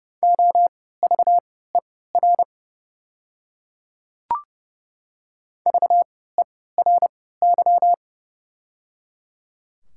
Morse code, heard as a keyed tone, about 30 wpm, spaced as for 17 wpm, 690 Hz. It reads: OVER VERY
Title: 3x Character Spacing - Top 200 Words - 30wpm